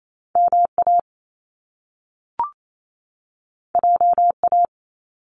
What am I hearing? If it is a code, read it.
Morse code, heard as a keyed tone, about 28 wpm, 700 Hz. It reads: MA JA
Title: Top Two Letter Combinations in Words 28wpm